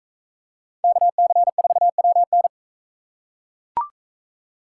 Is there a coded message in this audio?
Yes: Morse code from 0.8 to 2.5 s, heard as a keyed tone, about 40 wpm, 700 Hz. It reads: KC4WN